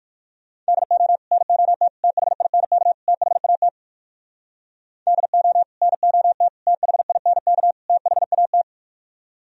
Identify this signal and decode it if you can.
Morse code, heard as a keyed tone, about 58 wpm, 700 Hz. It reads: DO NOT THINK THAT DO NOT THINK THAT